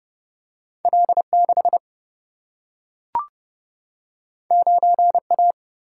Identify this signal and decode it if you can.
Morse code, heard as a keyed tone, about 30 wpm, 700 Hz. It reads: L 6 9 A
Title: Single-Letter Single-Number 30wpm